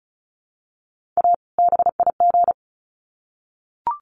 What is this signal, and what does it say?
Morse code, heard as a keyed tone, about 34 wpm, 700 Hz. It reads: A BIG
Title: Sets of 2 Words 35wpm